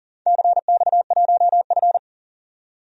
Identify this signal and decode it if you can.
Morse code, heard as a keyed tone, about 42 wpm, 700 Hz. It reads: CX1F
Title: DXCC-100 Call Signs - 4 Characters - 40wpm